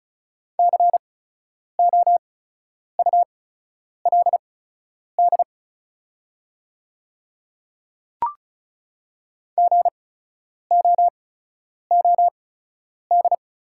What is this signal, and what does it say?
Morse code, heard as a keyed tone, about 36 wpm, spaced as for 10 wpm, 690 Hz. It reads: COULD GOOD
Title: 8x Character Spacing - Top 300 Words - 35wpm